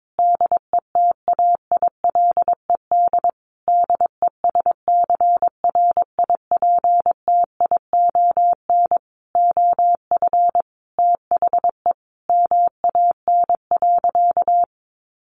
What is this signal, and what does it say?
Morse code, heard as a keyed tone, about 22 wpm, 700 Hz. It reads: DETAILED DESCRIPTION OF THE MAN.
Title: Sentences from Top 1500 Words 22wpm